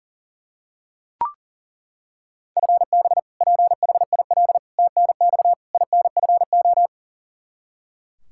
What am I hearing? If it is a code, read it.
Morse code, heard as a keyed tone, about 37 wpm, 700 Hz. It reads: FB PHIL TNX INFO
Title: Practice QSO - Exchange 3 - More Info - 40wpm